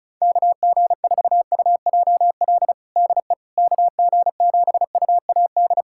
Morse code, heard as a keyed tone, about 34 wpm, 700 Hz. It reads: KG4UJL DE KG7UAD
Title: USA Callsign DE Callsign 35wpm